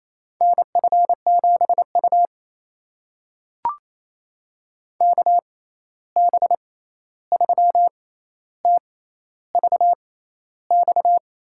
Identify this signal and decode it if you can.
Morse code, heard as a keyed tone, about 28 wpm, spaced as for 13 wpm, 700 Hz. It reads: NF7U KB3TVX
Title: USA Call Signs - 6x Character Spacing - 28wpm